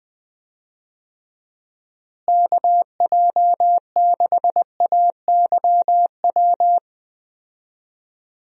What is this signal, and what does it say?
Morse code, heard as a keyed tone, about 19 wpm, 700 Hz. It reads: KJ6AYW
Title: USA Call Signs - 6 Characters - Repeated 3x - 20wpm